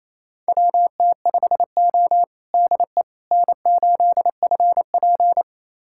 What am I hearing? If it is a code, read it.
Morse code, heard as a keyed tone, about 27 wpm, 700 Hz. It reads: WT5O DE N8FP
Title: USA Callsign DE Callsign - 4 Characters - 28wpm